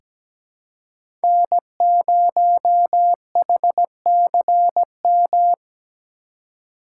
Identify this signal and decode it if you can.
Morse code, heard as a keyed tone, about 17 wpm, 700 Hz. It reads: N0HCM
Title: USA Call Signs - 5 Characters - 2x Character Spacing - 17wpm